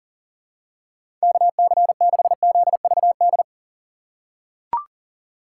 Morse code, heard as a keyed tone, about 37 wpm, 700 Hz. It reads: KC6ZVD
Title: USA Call Signs - 6 Characters - 2x Character Spacing - 40wpm